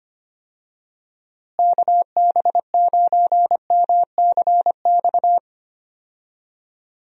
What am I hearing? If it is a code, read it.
Morse code, heard as a keyed tone, about 25 wpm, 700 Hz. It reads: KB9MCX
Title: USA Call Signs - 6 Characters - Repeated 3x - 25wpm